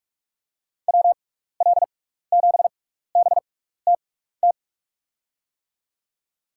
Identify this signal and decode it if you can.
Morse code, heard as a keyed tone, about 46 wpm, spaced as for 16 wpm, 700 Hz. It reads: WL7BTT
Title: USA Call Signs - 6x Character Spacing - 45wpm